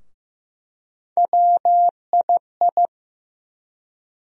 Morse code, heard as a keyed tone, about 15 wpm, 700 Hz. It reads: WII
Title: USA Call Sign Postfixes - 3 Letters 15wpm